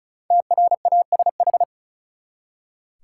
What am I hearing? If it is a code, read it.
Morse code, heard as a keyed tone, about 36 wpm, 700 Hz. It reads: TRASH
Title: Five Letter Words - 35wpm